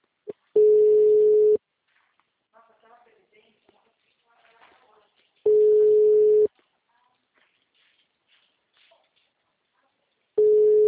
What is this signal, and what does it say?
Ringback tone (Europe, 425 Hz), with speech.
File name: out- -2010- - - .17434